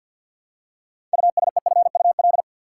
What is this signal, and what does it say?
Morse code, heard as a keyed tone, about 51 wpm, 700 Hz. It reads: USEFUL
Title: Six Letter Words - 50wpm